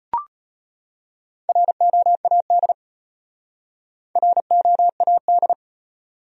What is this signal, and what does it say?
Morse code, heard as a keyed tone, about 31 wpm, 700 Hz. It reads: ROAD ROAD